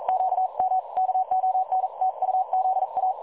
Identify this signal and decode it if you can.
Morse code, heard as a keyed tone, about 41 wpm, 750 Hz. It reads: QM KONTUZA
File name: dlazaawansowanych / nagraniazpasma / websdr recording 3565.0kHz